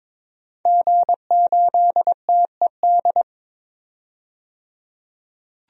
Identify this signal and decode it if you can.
Morse code, heard as a keyed tone, about 22 wpm, 700 Hz. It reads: G8TED
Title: DXCC-100 Call Signs - 5 Characters - 22wpm